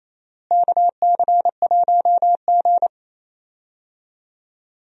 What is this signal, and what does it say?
Morse code, heard as a keyed tone, about 28 wpm, 700 Hz. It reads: KC1G
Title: USA Call Signs - 4 Characters - Repeated 2x - 28wpm